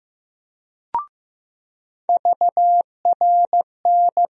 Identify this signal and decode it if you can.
Morse code, heard as a keyed tone, about 15 wpm, 700 Hz. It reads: VRN
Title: USA Call Sign Postfixes - 3 Letters 15wpm